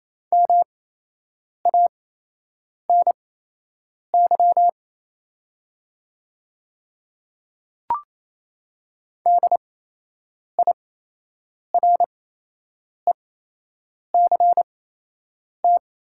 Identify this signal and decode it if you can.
Morse code, heard as a keyed tone, about 27 wpm, spaced as for 8 wpm, 700 Hz. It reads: MANY DIRECT
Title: 8x Character Spacing - Top 500 Words - 28wpm